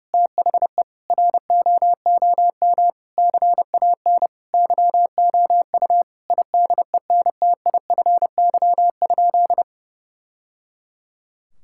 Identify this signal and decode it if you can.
Morse code, heard as a keyed tone, about 32 wpm, 700 Hz. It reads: THE ROOM CAN YOU IDENTIFY?
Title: Sentences from Top 1000 Words - Review Part 6 - 30wpm